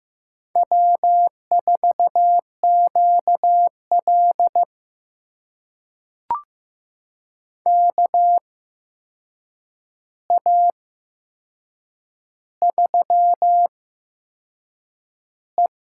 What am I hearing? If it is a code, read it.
Morse code, heard as a keyed tone, about 16 wpm, spaced as for 6 wpm, 700 Hz. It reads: W4QL KA3E
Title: USA Call Signs - 8x Character Spacing - 15wpm